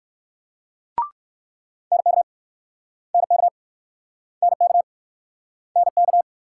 Morse code, heard as a keyed tone, about 61 wpm, 690 Hz. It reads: NK NK NK NK